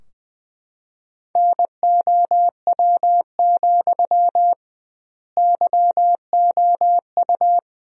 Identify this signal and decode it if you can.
Morse code, heard as a keyed tone, about 20 wpm, 700 Hz. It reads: NOW, YOU
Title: 2x Word-Spacing - Sentences from Top 200 Words - 20wpm